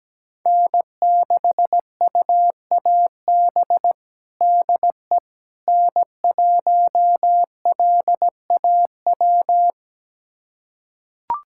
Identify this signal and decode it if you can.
Morse code, heard as a keyed tone, about 17 wpm, 700 Hz. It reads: N6UAB DE N1LAW
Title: USA Callsign DE Callsign - 5 Characters - 17wpm